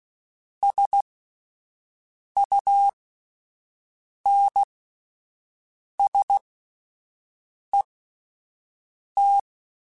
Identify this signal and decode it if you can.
Morse code, heard as a keyed tone, about 16 wpm, spaced as for 5 wpm, 800 Hz. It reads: SUNSET